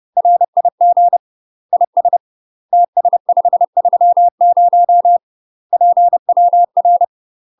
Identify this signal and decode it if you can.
Morse code, heard as a keyed tone, about 30 wpm, 700 Hz. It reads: RIG IS TS530 PWR
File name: UK MASTER QSOs TRAINING / 30WPM / UK MASTER QSOs 0038